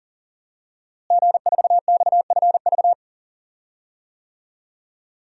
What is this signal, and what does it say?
Morse code, heard as a keyed tone, about 37 wpm, 700 Hz. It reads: G4XFV
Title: DXCC-100 Call Signs - 5 Characters - Repeated 3x - 40wpm